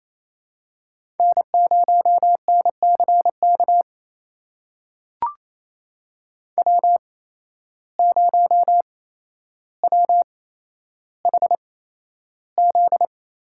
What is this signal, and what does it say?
Morse code, heard as a keyed tone, about 28 wpm, spaced as for 12 wpm, 700 Hz. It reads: N0NCK W0WHZ
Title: USA Call Signs - 5 Characters - 8x Character Spacing - 28wpm